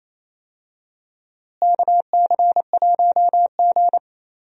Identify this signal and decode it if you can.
Morse code, heard as a keyed tone, about 28 wpm, 700 Hz. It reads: KC1G
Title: USA Call Signs - 4 Characters - Repeated 3x - 28wpm